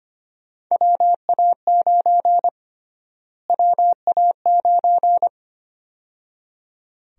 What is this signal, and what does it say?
Morse code, heard as a keyed tone, about 25 wpm, 700 Hz. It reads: WA9 WA9
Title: USA Call Sign Prefixes - Repeated 2x - 25wpm